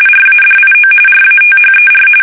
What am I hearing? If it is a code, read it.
RTTY, about 45.45 baud. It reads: RADIO RATHER
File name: fsk850 / teletype-RTTY